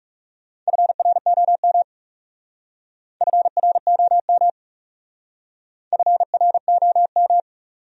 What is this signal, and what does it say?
Morse code, heard as a keyed tone, about 36 wpm, 700 Hz. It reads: FROM FROM FROM
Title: Speed-Racing Top 100 Words 30wpm